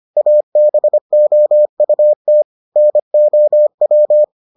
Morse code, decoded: ABOUT NOW

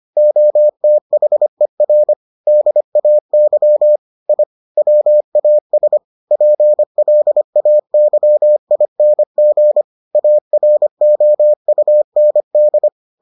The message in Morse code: OTHER DAY I WAS PLAYING AROUND

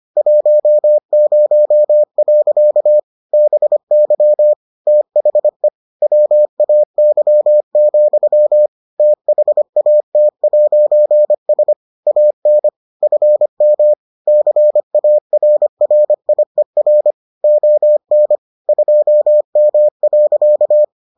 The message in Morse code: 10. BY THE WAY, THAT'S AN FM CARRIER ON 2M.

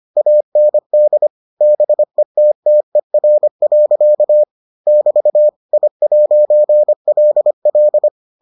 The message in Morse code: AND BETTER. <BT> I'LL

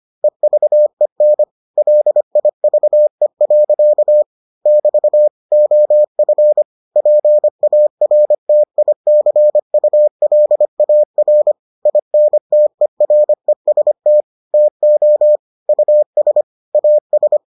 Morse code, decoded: EVEN LIVE. <BT> OF PARTICULAR INTEREST TO US AS

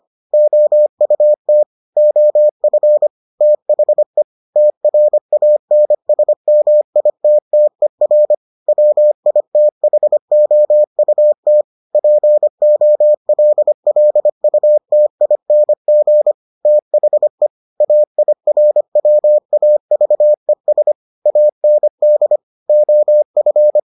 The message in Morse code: OUT OF THE TRANSMITTER WITHOUT POLLUTING THE AIRWAVES AND OF

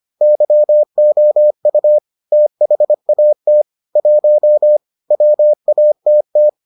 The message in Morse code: YOU THAT 1 WATT

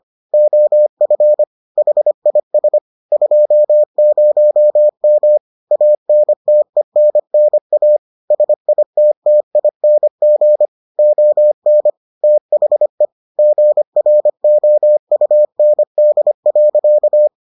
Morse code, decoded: OF HIS 20M ANTENNA SITTING ON THE GROUND.

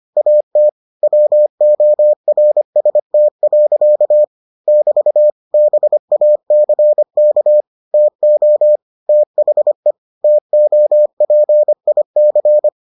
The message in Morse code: AT WORST. <BT> BACK TO THE TOPIC

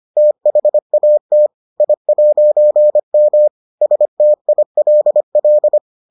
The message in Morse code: THAT I'M STILL

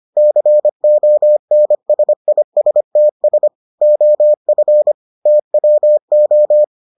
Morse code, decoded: CONSISTS OF TWO

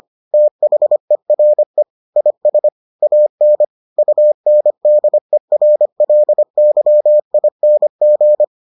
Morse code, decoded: THERE IS AN UNDERLYING

